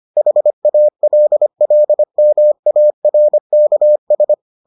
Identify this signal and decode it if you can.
Morse code, heard as a keyed tone, about 25 wpm, 600 Hz. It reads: HALLMARKS